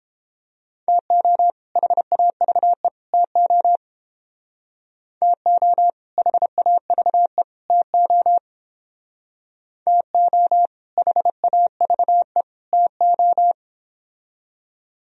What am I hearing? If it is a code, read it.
Morse code, heard as a keyed tone, about 31 wpm, 700 Hz. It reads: TO HAVE TO TO HAVE TO TO HAVE TO